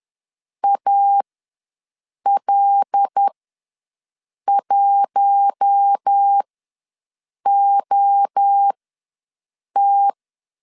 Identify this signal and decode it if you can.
Morse code, heard as a keyed tone, about 10 wpm, spaced as for 6 wpm, 800 Hz. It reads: AL1OT
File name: LEZIONI / 5 LBVC / 5 L ws 10wpm 3636